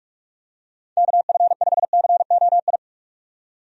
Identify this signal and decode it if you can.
Morse code, heard as a keyed tone, about 44 wpm, 700 Hz. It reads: KF5COI
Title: USA Call Signs - 6 Characters - 45wpm